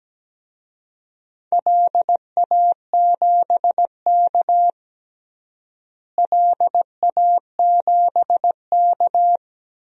Morse code, heard as a keyed tone, about 17 wpm, 700 Hz. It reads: LA7K LA7K
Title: DXCC-100 Call Signs - 4 Characters - Repeated 2x - 17wpm